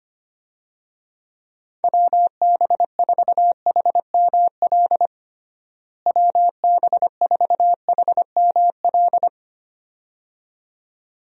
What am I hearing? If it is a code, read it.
Morse code, heard as a keyed tone, about 25 wpm, 700 Hz. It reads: WB4HML WB4HML
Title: USA Call Signs - 6 Characters - Repeated 2x - 25wpm